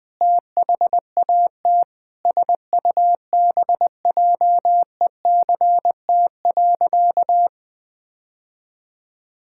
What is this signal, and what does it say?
Morse code, heard as a keyed tone, about 20 wpm, 700 Hz. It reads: THAT SUBJECT.
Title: Sentences from Top 1600 Words 20wpm